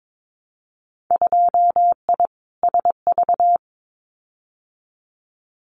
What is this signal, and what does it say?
Morse code, heard as a keyed tone, about 22 wpm, 700 Hz. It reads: 2I SV
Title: Winter Field Day Exchanges Encore 22wpm